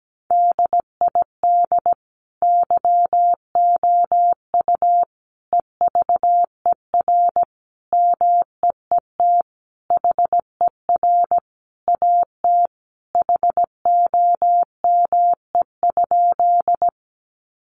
Morse code, heard as a keyed tone, about 17 wpm, 700 Hz. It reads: DID YOU EVER MEET HER AT HOME?
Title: Sentences from Top 1100 Words 17wpm